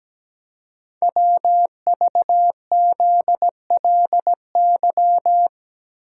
Morse code, heard as a keyed tone, about 17 wpm, 700 Hz. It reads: WVZLY
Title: Code Groups - 5 Letters - 17wpm